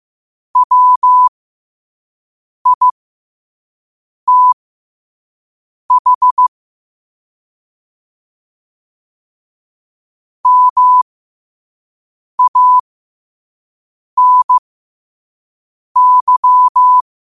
Morse code, heard as a keyed tone, about 15 wpm, spaced as for 5 wpm, 1000 Hz. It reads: WITH MANY
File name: qotdfiles / 5wpm